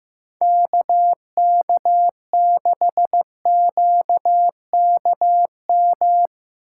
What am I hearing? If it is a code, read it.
Morse code, heard as a keyed tone, about 16 wpm, 700 Hz. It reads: KK6QKM
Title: USA Call Signs - 6 Characters - 2x Character Spacing - 15wpm